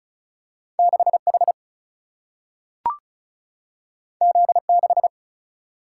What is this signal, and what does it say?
Morse code, heard as a keyed tone, about 36 wpm, 700 Hz. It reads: 6 H Z 6
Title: Single-Letter Single-Number 35wpm